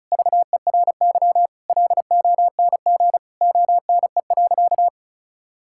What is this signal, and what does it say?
Morse code, heard as a keyed tone, about 34 wpm, 700 Hz. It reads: VERY LONG ONE.
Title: Sentences from Top 1000 Words - Review Part 5 - 35wpm